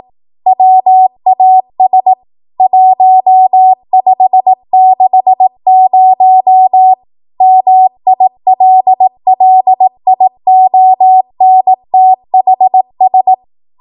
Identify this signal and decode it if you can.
Morse code, heard as a keyed tone, about 18 wpm, 750 Hz. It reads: WAS 1560 MILLIONTHS
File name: Archive / 18 WPM / 18WPM